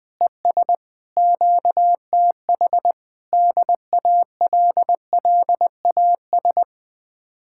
Morse code, heard as a keyed tone, about 20 wpm, 700 Hz. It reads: ES QTH DALLAS
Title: Speed-Racing Common QSO Phrases Encore - 17wpm